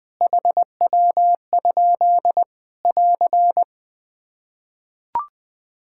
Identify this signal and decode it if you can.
Morse code, heard as a keyed tone, about 20 wpm, 700 Hz. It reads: HW? <AR>
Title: Practice QSO - Exchange 1 - Beginning - 20wpm